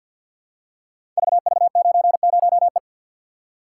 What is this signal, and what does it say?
Morse code, heard as a keyed tone, about 50 wpm, 700 Hz. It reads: VV90E
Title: Code Groups - 5 Letter:Numbers - 50wpm